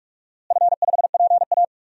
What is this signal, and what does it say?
Morse code, heard as a keyed tone, about 43 wpm, 700 Hz. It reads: F5PA
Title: DXCC-100 Call Signs - 4 Characters - Repeated 2x - 45wpm